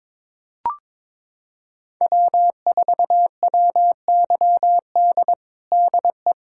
Morse code, heard as a keyed tone, about 22 wpm, 700 Hz. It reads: W4WYD DE
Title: USA Callsign DE Callsign - 5 Characters - 22wpm